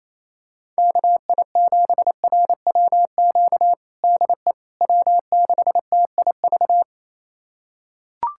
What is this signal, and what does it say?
Morse code, heard as a keyed tone, about 28 wpm, 700 Hz. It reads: KI7RWQ DE W6TIV
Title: USA Callsign DE Callsign 28wpm